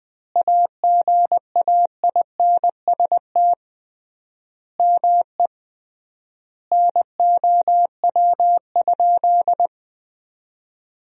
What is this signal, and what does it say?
Morse code, heard as a keyed tone, about 20 wpm, 700 Hz. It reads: AGAINST ME NOW?
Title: 3x Word-Spacing - Sentences from Top 400 Words - 20wpm